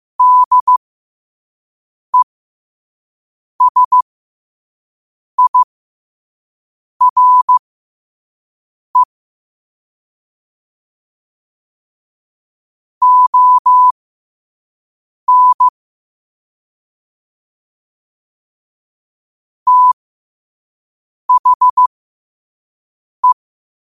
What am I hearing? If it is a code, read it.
Morse code, heard as a keyed tone, about 15 wpm, spaced as for 5 wpm, 1000 Hz. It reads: DESIRE ON THE
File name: qotdfiles / 5wpm